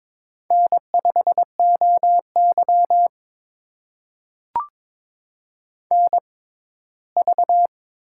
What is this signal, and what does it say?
Morse code, heard as a keyed tone, about 21 wpm, spaced as for 15 wpm, 700 Hz. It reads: N5OY NV
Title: USA Call Signs - 4 Characters - 6x Character Spacing - 22wpm